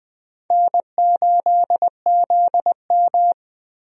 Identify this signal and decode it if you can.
Morse code, heard as a keyed tone, about 20 wpm, 700 Hz. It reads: N8ZM
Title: USA Call Signs - 4 Characters - Repeated 3x - 20wpm